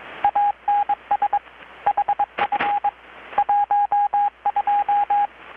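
Morse code, heard as a keyed tone, about 22 wpm, 820 Hz, with radio noise. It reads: ANS HF 12
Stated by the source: Morse from KPH's Night of Nights XX maritime radiotelegraphy event, received in Chiapas, Mexico about 3,800 km away on a Kenwood R-600 with a grounded 10 m long-wire antenna.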